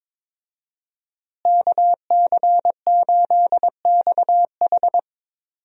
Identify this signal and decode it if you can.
Morse code, heard as a keyed tone, about 22 wpm, 700 Hz. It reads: KC8XH
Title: USA Call Signs - 5 Characters - Repeated 3x - 22wpm